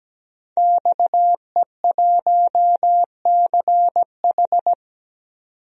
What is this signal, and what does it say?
Morse code, heard as a keyed tone, about 17 wpm, 700 Hz. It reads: XE1CH
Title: DXCC-100 Call Signs - 5 Characters - 17wpm